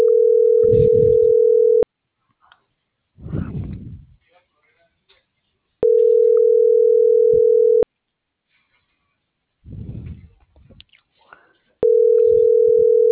Ringback tone (North America), 440 + 480 Hz.